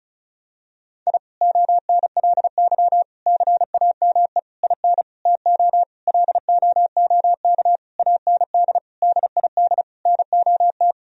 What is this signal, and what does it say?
Morse code, heard as a keyed tone, about 36 wpm, 700 Hz. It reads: I ONLY CAME IN TO LOOK AND DID NOT
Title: Sentences from Top 1000 Words - Review Part 2 - 35wpm